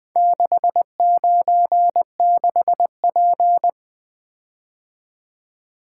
Morse code, heard as a keyed tone, about 23 wpm, 700 Hz. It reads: 696P
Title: Speed-Racing - Four Letter-Number Combinations - 20wpm